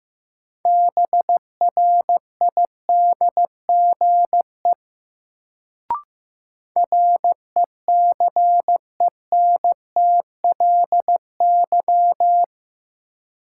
Morse code, heard as a keyed tone, about 15 wpm, 700 Hz. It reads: BRIDGE RECENTLY
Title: Sentences from Top 1200 Words - New Word Drill - 15wpm